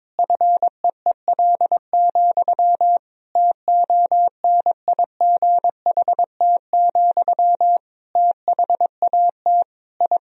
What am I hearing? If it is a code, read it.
Morse code, heard as a keyed tone, about 22 wpm, 700 Hz. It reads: FEEL, TONIGHT, THAT I
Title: Sentences from Top 1000 Words - Review Part 3 - 22wpm